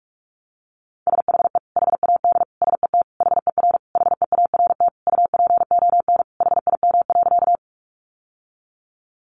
Morse code, heard as a keyed tone, about 44 wpm, 700 Hz. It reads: SHE HAD SET HER HEART UPON HIM.
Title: Sentences from Top 1200 Words 45wpm